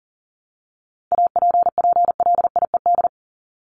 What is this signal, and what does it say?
Morse code, heard as a keyed tone, about 39 wpm, 700 Hz. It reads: APPLIED
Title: ED Suffix - Top 100 Words with Base Words 40wpm